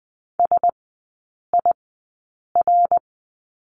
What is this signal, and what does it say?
Morse code, heard as a keyed tone, about 20 wpm, spaced as for 8 wpm, 700 Hz. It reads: SIR
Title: 053 Lesson 10e - Letter r - Three Characters 20wpm